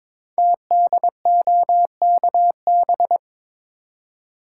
Morse code, heard as a keyed tone, about 23 wpm, 700 Hz. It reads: TDOKB
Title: Code Groups - 5 Letters - 22wpm